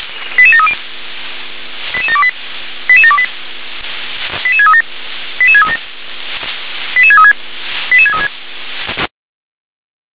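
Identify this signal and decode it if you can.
Five-tone selective call (ZVEI), heard as quick runs of five tones, tones of about 70 ms. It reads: 88730 88728 88728 88637 88637 88636 88636
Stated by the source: ZVEI five-tone call received on SDR, code 88702.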